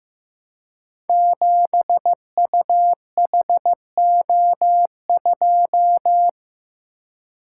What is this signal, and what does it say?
Morse code, heard as a keyed tone, about 15 wpm, 700 Hz. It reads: 7UHO2
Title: Code Groups - 5 Letter:Numbers - 15wpm